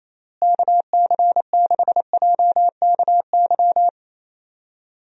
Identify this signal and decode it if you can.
Morse code, heard as a keyed tone, about 28 wpm, 700 Hz. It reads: KC6JKY
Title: USA Call Signs - 6 Characters - Repeated 2x - 28wpm